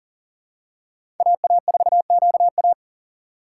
Morse code, heard as a keyed tone, about 39 wpm, 700 Hz. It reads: AA4QA
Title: USA Call Signs - 5 Characters - 3x Character Spacing - 40wpm